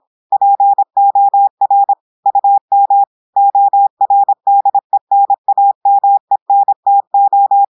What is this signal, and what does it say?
Morse code, heard as a keyed tone, about 26 wpm, 800 Hz. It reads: POR UM ORDENAMENTO
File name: audios / portugais / portugais texte en clair 1 vitesse 26 mots minutes